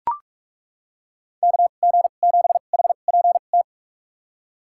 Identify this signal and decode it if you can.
Morse code, heard as a keyed tone, about 43 wpm, spaced as for 31 wpm, 700 Hz. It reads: KG7HPT
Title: USA Call Signs - 6 Characters - 2x Character Spacing - 45wpm